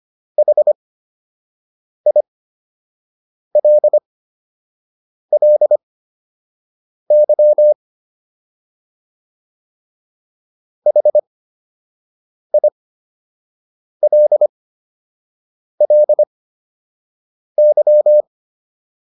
Morse code, heard as a keyed tone, about 25 wpm, spaced as for 6 wpm, 600 Hz. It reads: HILLY HILLY